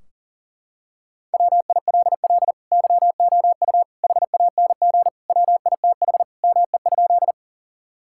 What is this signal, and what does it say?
Morse code, heard as a keyed tone, about 40 wpm, 700 Hz. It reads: WILL YOU HANG WITH ME?
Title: Sentences from Top 1600 Words 40wpm